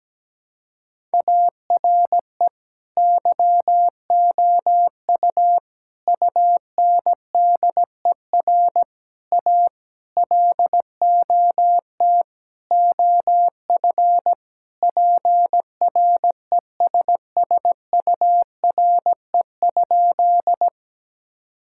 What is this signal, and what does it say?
Morse code, heard as a keyed tone, about 17 wpm, 700 Hz. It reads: ARE YOU UNDER A LOT OF PRESSURE?
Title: Sentences from Top 1000 Words - Review Part 6 - 17wpm